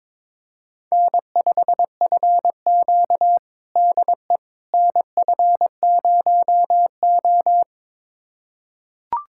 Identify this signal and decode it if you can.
Morse code, heard as a keyed tone, about 22 wpm, 700 Hz. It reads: N5FQ DE NF0O
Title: USA Callsign DE Callsign - 4 Characters - 22wpm